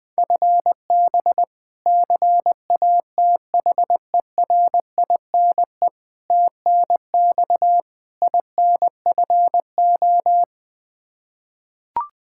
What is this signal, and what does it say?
Morse code, heard as a keyed tone, about 21 wpm, 700 Hz. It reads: FB CATHERINE TNX INFO
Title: K1USN SST QSO Practice 20wpm